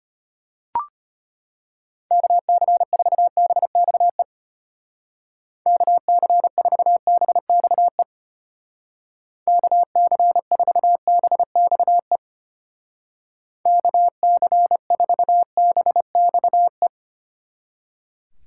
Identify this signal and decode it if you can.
Morse code, heard as a keyed tone, about 32 wpm, 700 Hz. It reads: KC4BXE KC4BXE KC4BXE KC4BXE